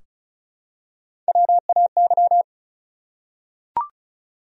Morse code, heard as a keyed tone, about 34 wpm, 700 Hz. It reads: WAY
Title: Top 100 Words Repeated 35wpm